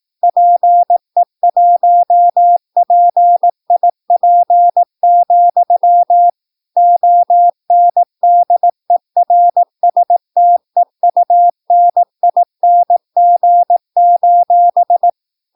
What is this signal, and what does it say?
Morse code, heard as a keyed tone, about 18 wpm, 700 Hz. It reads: PE1PIP, ONDERSTEUNING:
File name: PI4AA Uitzending